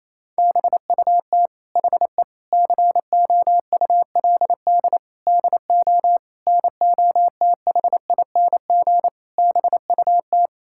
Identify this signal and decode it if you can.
Morse code, heard as a keyed tone, about 28 wpm, 700 Hz. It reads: BUT HE COULD DO NOTHING BUT
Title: Sentences from Top 1600 Words 28wpm